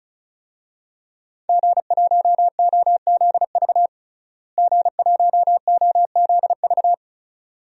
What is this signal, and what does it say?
Morse code, heard as a keyed tone, about 36 wpm, 700 Hz. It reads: G1OZV G1OZV
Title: DXCC-100 Call Signs - 5 Characters - Repeated 2x - 35wpm